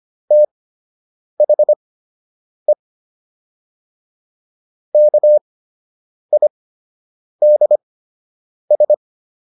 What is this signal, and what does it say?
Morse code, heard as a keyed tone, about 25 wpm, spaced as for 8 wpm, 600 Hz. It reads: THE KIDS